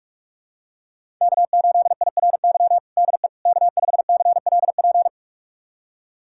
Morse code, heard as a keyed tone, about 46 wpm, 700 Hz. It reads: K8IRY DE K5CLP
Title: USA Callsign DE Callsign - 5 Characters - 45wpm